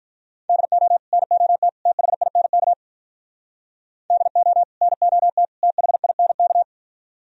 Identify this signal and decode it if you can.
Morse code, heard as a keyed tone, about 46 wpm, 700 Hz. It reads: DO NOT THINK DO NOT THINK